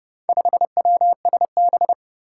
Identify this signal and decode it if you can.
Morse code, heard as a keyed tone, about 30 wpm, 700 Hz. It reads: 5WSB
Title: Four Challenging Letter-Number Combinations 30wpm